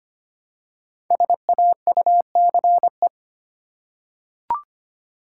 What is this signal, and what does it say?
Morse code, heard as a keyed tone, about 25 wpm, 700 Hz. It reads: SAUCE
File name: 5 Letter Words - First 3 Letters Given - 25wpm